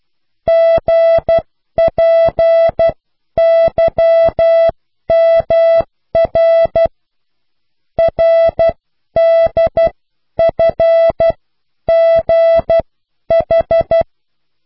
Morse code, heard as a keyed tone, about 12 wpm, spaced as for 10 wpm, 660 Hz. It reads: GPYMR RDFGH